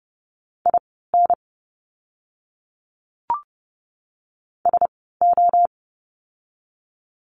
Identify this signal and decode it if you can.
Morse code, heard as a keyed tone, about 29 wpm, spaced as for 17 wpm, 700 Hz. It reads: IN SO
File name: course / 30:10wpm / 230 Lesson 37f - Letter x - Words 30wpm